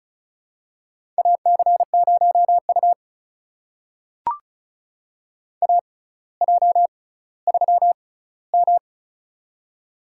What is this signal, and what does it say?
Morse code, heard as a keyed tone, about 36 wpm, spaced as for 18 wpm, 700 Hz. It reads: AC0U AJ3M
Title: USA Call Signs - 4 Characters - 6x Character Spacing - 35wpm